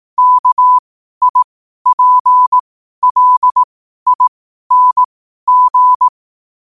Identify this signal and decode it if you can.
Morse code, heard as a keyed tone, about 18 wpm, spaced as for 12 wpm, 1000 Hz. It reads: KIPLING
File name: qotdfiles / 12wpm